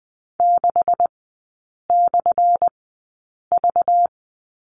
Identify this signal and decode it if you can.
Morse code, heard as a keyed tone, about 20 wpm, spaced as for 8 wpm, 700 Hz. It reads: 6/V
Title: 205 Lesson 33e - Letter j - Three Characters 20wpm